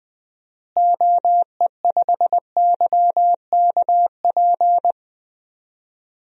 Morse code, heard as a keyed tone, about 20 wpm, 700 Hz. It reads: OE5YKP
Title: DXCC-100 Call Signs - 6 Characters - Repeated 2x - 20wpm